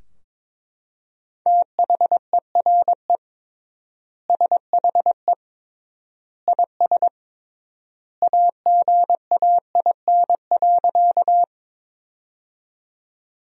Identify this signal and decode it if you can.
Morse code, heard as a keyed tone, about 22 wpm, 700 Hz. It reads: THERE SHE IS AGAIN.
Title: 3x Word-Spacing - Sentences from Top 200 Words - 22wpm